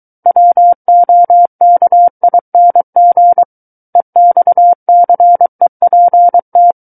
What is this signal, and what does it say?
Morse code, heard as a keyed tone, about 23 wpm, 700 Hz. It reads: WOKING EXCEPT
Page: SKCC Learning Center - Morse Code Practice Files